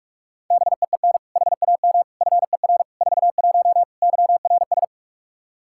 Morse code, heard as a keyed tone, about 46 wpm, 700 Hz. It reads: BEEN HAM FER 41 YRS
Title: K1USN SST QSO Practice 45wpm